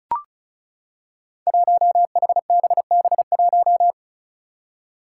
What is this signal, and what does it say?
Morse code, heard as a keyed tone, about 36 wpm, 700 Hz. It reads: 1HBB1
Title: Five Challenging Letter-Number Combinations 35wpm